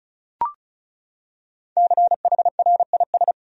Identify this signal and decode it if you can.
Morse code, heard as a keyed tone, about 36 wpm, 700 Hz. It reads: CHRIS
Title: USA Common Names - 35wpm